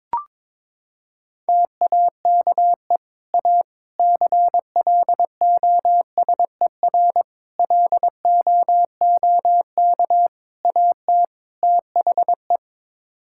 Morse code, heard as a keyed tone, about 22 wpm, 700 Hz. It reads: TAKE A CLOSER LOOK AT THE
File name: practice / 22wpm / Sets of 6 Words - 22wpm